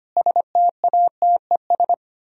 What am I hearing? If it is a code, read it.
Morse code, heard as a keyed tone, about 30 wpm, 700 Hz. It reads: STATES
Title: Speed-Racing - Sets of 5 Words 25wpm